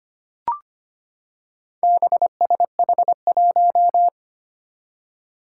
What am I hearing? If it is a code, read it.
Morse code, heard as a keyed tone, about 25 wpm, 700 Hz. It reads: BSH1
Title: Four Challenging Letter-Number Combinations 25wpm